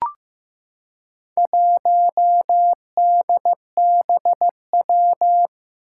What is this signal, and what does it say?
Morse code, heard as a keyed tone, about 15 wpm, 700 Hz. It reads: 1DBW
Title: Four Challenging Letter-Number Combinations 15wpm